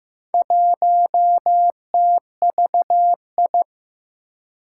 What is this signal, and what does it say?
Morse code, heard as a keyed tone, about 15 wpm, 700 Hz. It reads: 1TVI
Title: Four Challenging Letter-Number Combinations 15wpm